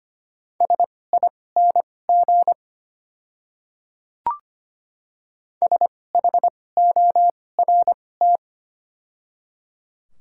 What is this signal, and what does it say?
Morse code, heard as a keyed tone, about 25 wpm, spaced as for 18 wpm, 700 Hz. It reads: SING SHORT
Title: 2x Character Spacing - Top 500 Words - 25wpm